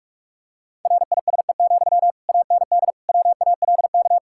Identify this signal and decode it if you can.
Morse code, heard as a keyed tone, about 46 wpm, 700 Hz. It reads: RISE, AND WALK
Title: Sentences from Top 1000 Words - Review Part 6 - 45wpm